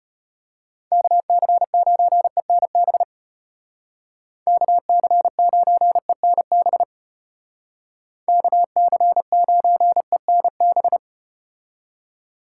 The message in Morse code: KC9ENB KC9ENB KC9ENB

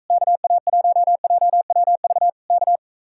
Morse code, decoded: KA1JWV K

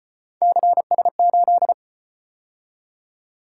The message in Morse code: CS8

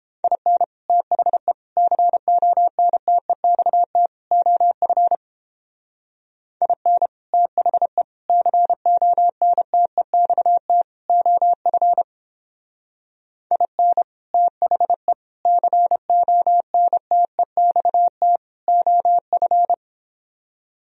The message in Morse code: IN THE CONTEXT OF IN THE CONTEXT OF IN THE CONTEXT OF

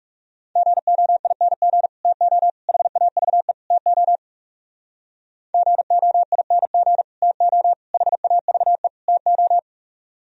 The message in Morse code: GOING TO HAVE TO GOING TO HAVE TO

